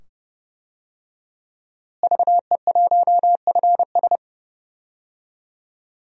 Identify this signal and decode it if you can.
Morse code, heard as a keyed tone, about 30 wpm, 700 Hz. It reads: VE1FS
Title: DXCC-100 Call Signs - 5 Characters - Repeated 2x - 30wpm